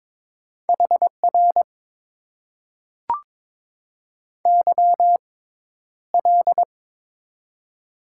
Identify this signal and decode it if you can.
Morse code, heard as a keyed tone, about 22 wpm, spaced as for 11 wpm, 700 Hz. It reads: HR YL
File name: Top 90 QSO Elements - 6x Character Spacing - 22wpm